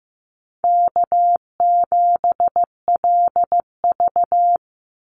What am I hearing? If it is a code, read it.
Morse code, heard as a keyed tone, about 15 wpm, 700 Hz. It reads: K7LV
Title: Call Signs Repeated 2x - 15wpm